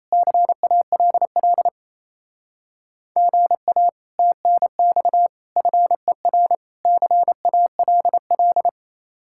Morse code, heard as a keyed tone, about 35 wpm, 700 Hz. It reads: CALL GA TNX FER CALL
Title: Speed-Racing Common QSO Phrases Encore - 28wpm